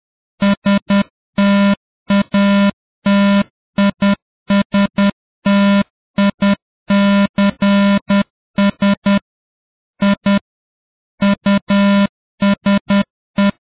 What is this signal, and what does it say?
Morse code, heard as a keyed tone, about 10 wpm, 200 Hz. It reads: STATISTICS I USE